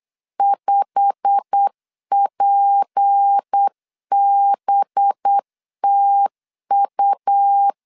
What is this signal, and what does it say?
Morse code, heard as a keyed tone, about 8 wpm, 800 Hz. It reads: 5PBTU
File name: LEZIONI / 8 YJ45 / 8 YJ45 8wpm 3214